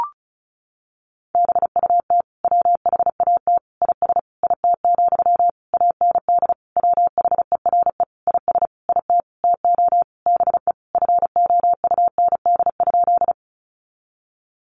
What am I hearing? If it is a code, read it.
Morse code, heard as a keyed tone, about 34 wpm, 700 Hz. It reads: BUT WHAT IS IT, AND WHERE IS IT TO BE FOUND?
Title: Sentences from Top 300 Words 35wpm